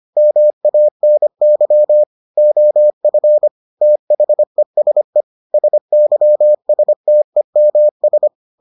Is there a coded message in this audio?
Yes: Morse code from 0.2 to 8.3 s, heard as a keyed tone, about 25 wpm, 600 Hz. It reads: MANY OF THESE SYSTEMS